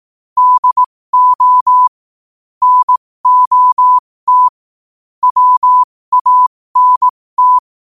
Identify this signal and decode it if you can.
Morse code, heard as a keyed tone, about 18 wpm, spaced as for 15 wpm, 1000 Hz. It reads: DO NOT WANT
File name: qotdfiles / 15wpm